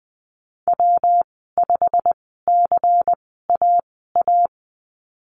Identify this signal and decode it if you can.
Morse code, heard as a keyed tone, about 20 wpm, spaced as for 14 wpm, 700 Hz. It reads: W5CAA
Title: 128 Lesson 21g - Number 6 - Call Signs 20wpm